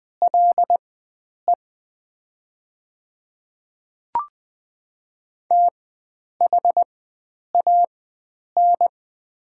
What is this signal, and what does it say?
Morse code, heard as a keyed tone, about 20 wpm, spaced as for 9 wpm, 700 Hz. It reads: LE THAN
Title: 4x Character Spacing - Top 300 Words - 20wpm